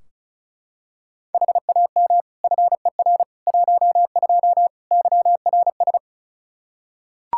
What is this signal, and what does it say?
Morse code, heard as a keyed tone, about 36 wpm, 700 Hz. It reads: HAM FER 12 YRS
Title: Practice QSO - Exchange 3 - More Info - 35wpm